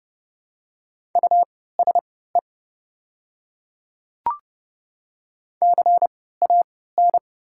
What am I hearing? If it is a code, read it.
Morse code, heard as a keyed tone, about 30 wpm, spaced as for 17 wpm, 700 Hz. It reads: USE CAN
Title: 3x Character Spacing - Top 100 Words - 30wpm